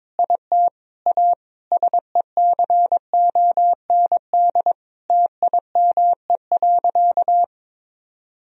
Morse code, heard as a keyed tone, about 22 wpm, 700 Hz. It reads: IT A SECOND TIME.
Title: Sentences from Top 1000 Words - Review Part 1 - 22wpm